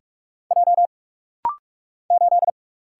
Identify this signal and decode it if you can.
Morse code, heard as a keyed tone, about 44 wpm, 700 Hz. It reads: J 8
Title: Single Letter-Number Alt - Rapid-Fire 45wpm